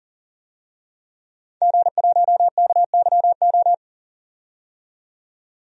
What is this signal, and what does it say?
Morse code, heard as a keyed tone, about 40 wpm, 700 Hz. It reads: G1KYO
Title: DXCC-100 Call Signs - 5 Characters - 40wpm